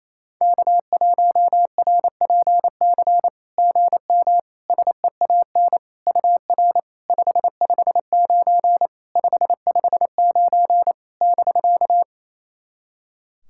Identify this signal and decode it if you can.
Morse code, heard as a keyed tone, about 28 wpm, 700 Hz. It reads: K1RPC GM SEAN UR 559 559 <BK>
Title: SOTA QSO Practice 28wpm